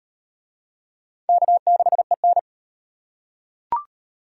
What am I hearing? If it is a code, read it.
Morse code, heard as a keyed tone, about 38 wpm, 700 Hz. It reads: K6EN